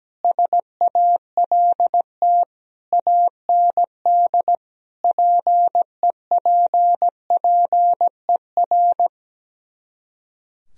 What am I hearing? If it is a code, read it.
Morse code, heard as a keyed tone, about 17 wpm, 700 Hz. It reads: SALT AND PEPPER
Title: Speed-Racing - Sets of 5 Words 17wpm